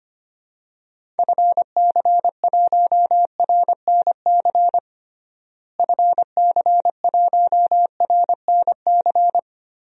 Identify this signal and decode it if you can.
Morse code, heard as a keyed tone, about 25 wpm, 700 Hz. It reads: FC1RNC FC1RNC
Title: DXCC-100 Call Signs - 6 Characters - Repeated 3x - 25wpm